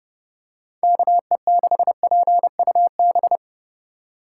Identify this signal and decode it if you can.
Morse code, heard as a keyed tone, about 29 wpm, 700 Hz. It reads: KE6PUB